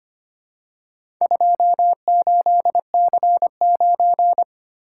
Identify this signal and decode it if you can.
Morse code, heard as a keyed tone, about 26 wpm, 690 Hz. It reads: 28C9
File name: practice / 25wpm / Four Letter-Number Combinations - 25wpm